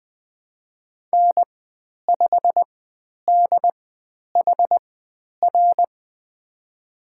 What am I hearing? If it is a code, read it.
Morse code, heard as a keyed tone, about 20 wpm, spaced as for 10 wpm, 700 Hz. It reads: N5DHR
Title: 095 Lesson 16g - Letter c - Call Signs 20wpm